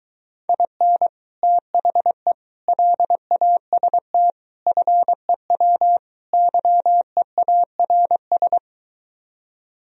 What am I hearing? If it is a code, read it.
Morse code, heard as a keyed tone, about 23 wpm, 700 Hz. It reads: IN THE LAST FEW YEARS